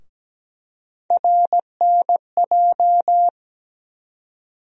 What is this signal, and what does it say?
Morse code, heard as a keyed tone, about 17 wpm, 700 Hz. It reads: RNJ
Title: USA Call Sign Postfixes - 3 Letters 17wpm